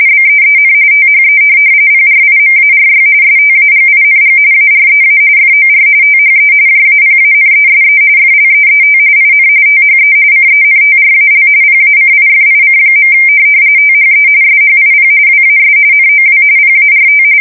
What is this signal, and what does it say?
RTTY, about 45.45 baud. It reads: BEEN THE PRIMARY STAGING AREA FOR THE JAPANESE FLEET IN THE CENTRAL PACIFIC. THE SHIP WAS NOW PART